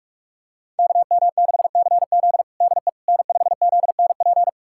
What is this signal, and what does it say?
Morse code, heard as a keyed tone, about 42 wpm, 700 Hz. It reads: KM6CZ DE N5ZNP
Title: USA Callsign DE Callsign - 5 Characters - 45wpm